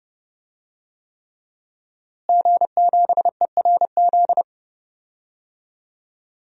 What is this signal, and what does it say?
Morse code, heard as a keyed tone, about 30 wpm, 700 Hz. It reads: G7ERZ
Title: DXCC-100 Call Signs - 5 Characters - Repeated 3x - 30wpm